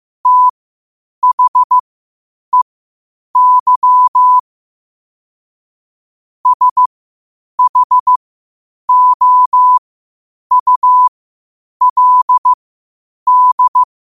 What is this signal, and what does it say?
Morse code, heard as a keyed tone, about 15 wpm, spaced as for 8 wpm, 1000 Hz. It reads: THEY SHOULD
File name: qotdfiles / 8wpm